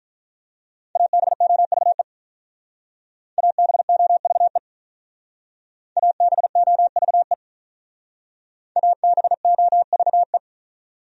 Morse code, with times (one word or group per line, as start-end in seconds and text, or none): 1.0-2.0 ABOVE
3.4-4.6 ABOVE
6.0-7.3 ABOVE
8.8-10.4 ABOVE